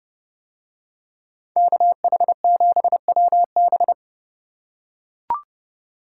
Morse code, heard as a keyed tone, about 29 wpm, 700 Hz. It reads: KH7WB